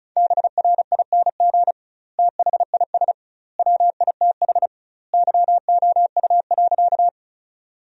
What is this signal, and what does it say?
Morse code, heard as a keyed tone, about 36 wpm, 700 Hz. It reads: BRING THIS WITH YOU.
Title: 2x Word-Spacing - Sentences from Top 500 Words - 35wpm